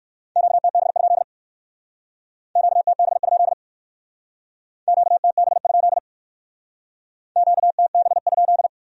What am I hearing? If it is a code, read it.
Morse code, heard as a keyed tone, about 56 wpm, 700 Hz. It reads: QTB? QTB? QTB? QTB?
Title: Speed-Racing Q-Codes - 45wpm